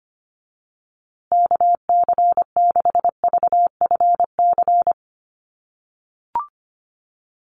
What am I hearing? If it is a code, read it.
Morse code, heard as a keyed tone, about 25 wpm, 700 Hz. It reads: KC6VFC
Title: USA Call Signs Repeated 2x 25wpm